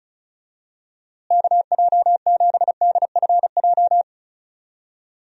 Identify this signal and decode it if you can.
Morse code, heard as a keyed tone, about 35 wpm, 700 Hz. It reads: KJ7DFJ
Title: USA Call Signs - 6 Characters - Repeated 2x - 35wpm